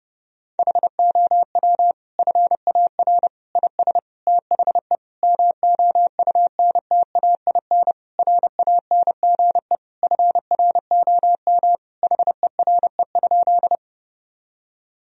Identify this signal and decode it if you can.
Morse code, heard as a keyed tone, about 30 wpm, 700 Hz. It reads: HOW FAR IS THE MOUNTAIN RANGE FROM HERE?
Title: Sentences from Top 1000 Words - Review Part 4 - 30wpm